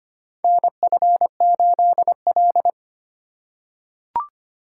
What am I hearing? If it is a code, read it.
Morse code, heard as a keyed tone, about 25 wpm, 700 Hz. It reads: NF8L
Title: USA Call Signs - 4 Characters - 3x Character Spacing - 25wpm